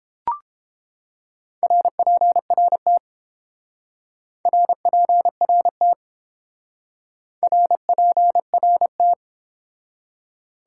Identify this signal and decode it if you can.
Morse code, heard as a keyed tone, about 29 wpm, 700 Hz. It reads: RPRT RPRT RPRT